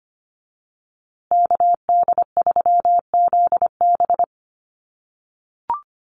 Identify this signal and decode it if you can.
Morse code, heard as a keyed tone, about 25 wpm, 700 Hz. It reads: KD3ZB
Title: USA Call Signs Repeated 2x 25wpm